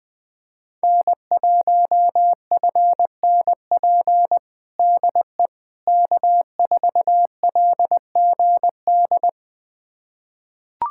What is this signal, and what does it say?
Morse code, heard as a keyed tone, about 20 wpm, 700 Hz. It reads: N1FNP DE K4LGD
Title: USA Callsign DE Callsign - 5 Characters - 20wpm